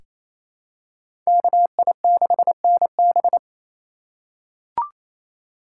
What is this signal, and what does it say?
Morse code, heard as a keyed tone, about 29 wpm, 690 Hz. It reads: KI6NB
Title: USA Call Signs - 3x Character Spacing - 28wpm